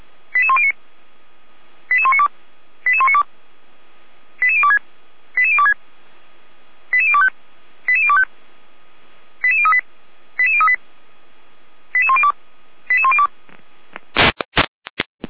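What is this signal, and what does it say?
Five-tone selective call (ZVEI), heard as quick runs of five tones, tones of about 80 ms. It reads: 88109 88192 88192 88026 88026 88025 88025 88038 88038 88102 88102
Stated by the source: ZVEI five-tone selective call received by SDR, fire brigade alerting, Feuerwehr Orlamünde.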